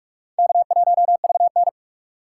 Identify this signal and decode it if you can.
Morse code, heard as a keyed tone, about 41 wpm, 700 Hz. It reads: K1VN
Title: USA Call Signs - 4 Characters - 2x Character Spacing - 45wpm